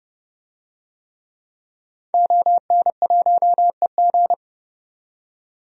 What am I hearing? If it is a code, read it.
Morse code, heard as a keyed tone, about 31 wpm, 700 Hz. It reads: ON1EG
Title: DXCC-100 Call Signs - 5 Characters - 30wpm